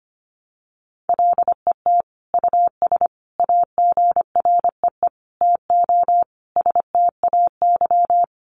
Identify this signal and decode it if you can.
Morse code, heard as a keyed tone, about 25 wpm, 700 Hz. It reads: LET US AGREE TO STAY
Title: Sentences from Top 1100 Words 25wpm